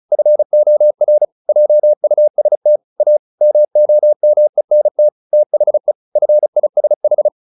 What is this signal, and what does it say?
Morse code, heard as a keyed tone, about 34 wpm, 600 Hz. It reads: FOR JUST A MOMENT THE FISH
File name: the old man and the sea 2 35wpm